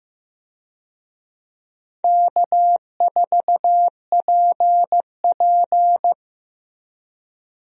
Morse code, heard as a keyed tone, about 16 wpm, 700 Hz. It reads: K4PP
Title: USA Call Signs - 4 Characters - Repeated 3x - 15wpm